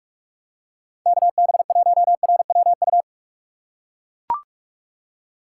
Morse code, heard as a keyed tone, about 44 wpm, 700 Hz. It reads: KB1RWU
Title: USA Call Signs - 6 Characters - 6x Character Spacing - 45wpm